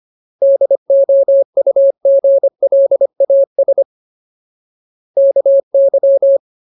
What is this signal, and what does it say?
Morse code, heard as a keyed tone, about 25 wpm, 550 Hz. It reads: DOUGLAS KY
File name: Ultimate Word List / 25WPM Ultimate